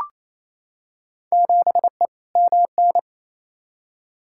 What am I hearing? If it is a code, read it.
Morse code, heard as a keyed tone, about 28 wpm, 700 Hz. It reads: 7E MN
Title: Field Day Exchanges Encore - 28wpm